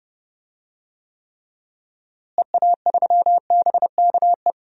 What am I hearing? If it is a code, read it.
Morse code, heard as a keyed tone, about 30 wpm, 700 Hz. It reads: EA3BKE
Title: DXCC-100 Call Signs - 6 Characters - Repeated 3x - 30wpm